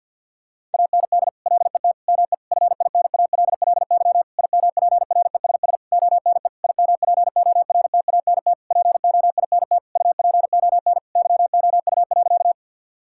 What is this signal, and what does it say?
Morse code, heard as a keyed tone, about 50 wpm, 700 Hz. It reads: AND LET ME FINALLY IMPRESS ONE IMPORTANT POINT UPON YOU.
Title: Sentences from Top 1600 Words 50wpm